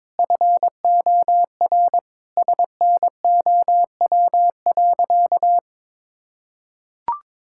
Morse code, heard as a keyed tone, about 22 wpm, 700 Hz. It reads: FOR SNOW.
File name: practice / 22wpm / Sentences with 4-Word Phrases - 22wpm